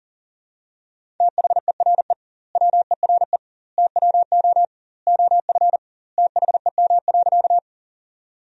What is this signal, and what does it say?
Morse code, heard as a keyed tone, about 40 wpm, 700 Hz. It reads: THERE WERE TWO OF THEM.
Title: 2x Word-Spacing - Sentences from Top 100 Words - 40wpm